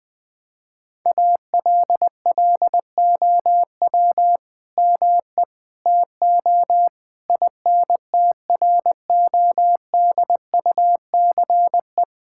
Morse code, heard as a keyed tone, about 20 wpm, 700 Hz. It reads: ALLOW ME TO INTRODUCE
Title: Sentences from Top 1600 Words 20wpm